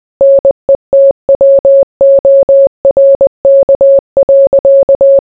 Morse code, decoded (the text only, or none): NETWORK.